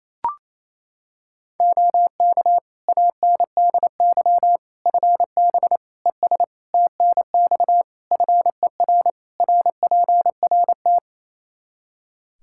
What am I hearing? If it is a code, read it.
Morse code, heard as a keyed tone, about 28 wpm, 700 Hz. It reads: OK ANDY FB ES TNX FER RPRT
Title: K1USN SST QSO Practice 28wpm